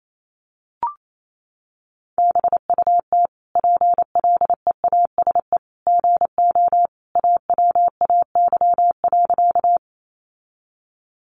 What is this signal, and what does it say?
Morse code, heard as a keyed tone, about 28 wpm, 700 Hz. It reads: BUT PLEASE GO AWAY.
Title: Sentences from Top 1500 Words 28wpm